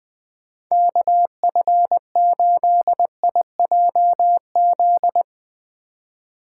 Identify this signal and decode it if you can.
Morse code, heard as a keyed tone, about 22 wpm, 700 Hz. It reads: KF8IJZ
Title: Speed Racing - USA 2x3 Call Signs 20wpm